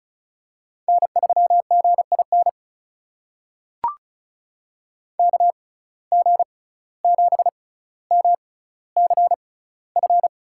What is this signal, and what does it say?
Morse code, heard as a keyed tone, about 36 wpm, spaced as for 17 wpm, 700 Hz. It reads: N3GIN KG7MCF
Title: USA Call Signs - 6x Character Spacing - 35wpm